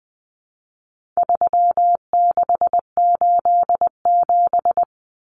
Morse code, heard as a keyed tone, about 20 wpm, 700 Hz. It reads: 3687
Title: Numbers - 4-Digit 20wpm